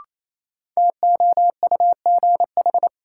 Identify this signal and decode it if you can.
Morse code, heard as a keyed tone, about 28 wpm, 700 Hz. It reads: TOUGH
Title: Sentences from Top 1200 Words - New Word Drill Alt - 28wpm